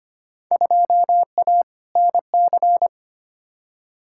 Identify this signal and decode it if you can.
Morse code, heard as a keyed tone, about 25 wpm, 700 Hz. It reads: 2A NC
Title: Field Day Exchanges Encore - 25wpm